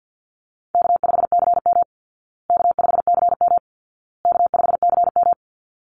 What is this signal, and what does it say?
Morse code, heard as a keyed tone, about 50 wpm, 700 Hz. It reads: K5CM K5CM K5CM
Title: Easy Call Signs Repeated 3x - 50wpm